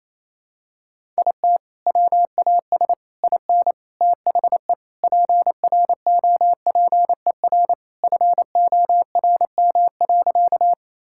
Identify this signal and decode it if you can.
Morse code, heard as a keyed tone, about 28 wpm, 700 Hz. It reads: IT WAS IN THE PROPER FORM.
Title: Sentences from Top 1000 Words - Review Part 5 - 28wpm